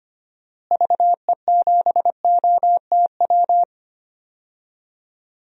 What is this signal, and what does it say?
Morse code, heard as a keyed tone, about 25 wpm, 700 Hz. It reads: VE7OTW
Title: DXCC-100 Call Signs - 6 Characters - Repeated 3x - 25wpm